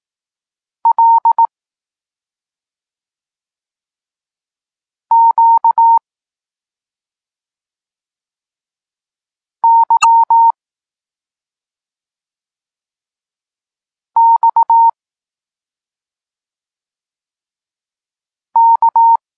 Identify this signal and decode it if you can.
Morse code, heard as a keyed tone, about 18 wpm, 920 Hz. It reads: L Q Y X K
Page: Tekst 32 – Doskonalenie odbioru poznanych liter